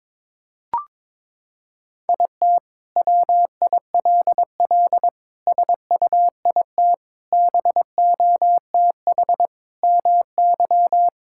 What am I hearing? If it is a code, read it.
Morse code, heard as a keyed tone, about 21 wpm, 700 Hz. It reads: IT WILL SUIT BOTH MY
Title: Sentences from Top 1600 Words 22wpm